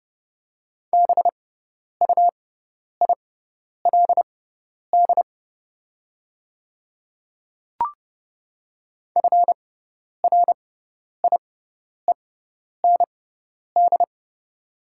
Morse code, heard as a keyed tone, about 30 wpm, spaced as for 10 wpm, 700 Hz. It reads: BUILD FRIEND
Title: 6x Character Spacing - Top 300 Words - 30wpm